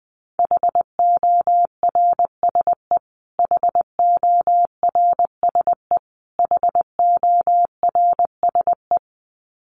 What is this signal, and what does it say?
Morse code, heard as a keyed tone, about 20 wpm, 700 Hz. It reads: HORSE HORSE HORSE
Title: Top 500 Words Repeated 20wpm